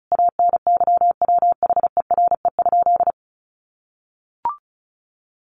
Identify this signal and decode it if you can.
Morse code, heard as a keyed tone, about 34 wpm, 700 Hz. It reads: ANYWHERE?
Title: Sentences from Top 1400 Words 35wpm